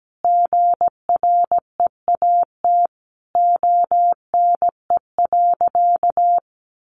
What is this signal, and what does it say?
Morse code, heard as a keyed tone, about 17 wpm, 700 Hz. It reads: GREAT ONE.
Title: Sentences from Top 1300 Words 17wpm